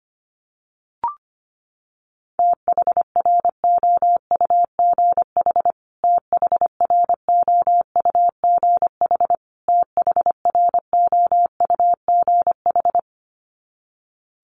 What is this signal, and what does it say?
Morse code, heard as a keyed tone, about 25 wpm, 700 Hz. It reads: THROUGH THROUGH THROUGH
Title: Top 500 Words Repeated 25wpm